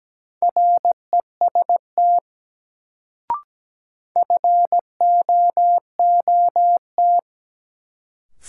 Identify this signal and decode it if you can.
Morse code, heard as a keyed tone, about 17 wpm, 700 Hz. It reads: REST FOOT
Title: Sentences from Top 500 Words - New Word Drill - 17wpm